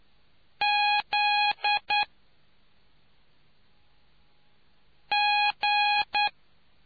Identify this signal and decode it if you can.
Morse code, heard as a keyed tone, about 9 wpm, 830 Hz. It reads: Z G